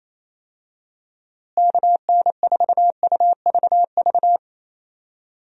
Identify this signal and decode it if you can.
Morse code, heard as a keyed tone, about 28 wpm, 700 Hz. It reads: KN4UVV
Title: USA Call Signs - 6 Characters - Repeated 2x - 28wpm